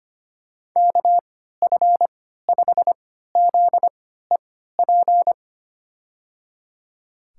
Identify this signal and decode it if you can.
Morse code, heard as a keyed tone, about 25 wpm, spaced as for 14 wpm, 700 Hz. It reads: KF5ZEP
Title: USA Call Signs - 6 Characters - 3x Character Spacing - 25wpm